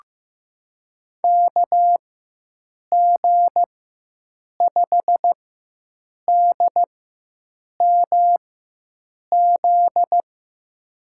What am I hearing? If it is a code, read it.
Morse code, heard as a keyed tone, about 16 wpm, spaced as for 7 wpm, 700 Hz. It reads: KG5DMZ
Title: USA Call Signs - 6 Characters - 4x Character Spacing - 15wpm